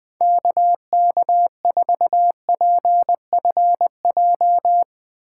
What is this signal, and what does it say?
Morse code, heard as a keyed tone, about 19 wpm, 700 Hz. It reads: KK4PFJ
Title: Speed Racing - USA 2x3 Call Signs 17wpm